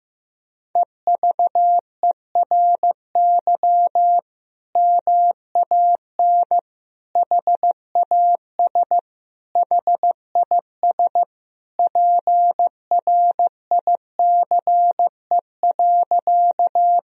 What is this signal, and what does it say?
Morse code, heard as a keyed tone, about 15 wpm, 700 Hz. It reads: EVERY MAN HAS HIS PRICE.
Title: Sentences from Top 1000 Words - Review Part 4 - 15wpm